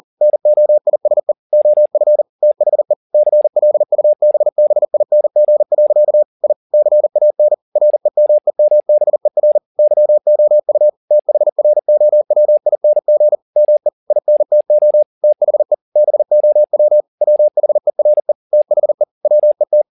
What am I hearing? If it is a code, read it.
Morse code, heard as a keyed tone, about 40 wpm, 600 Hz. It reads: NOISE OF THE CLUBBING. I CAN REMEMBER YOU THROWING ME INTO THE BOW WHERE THE WET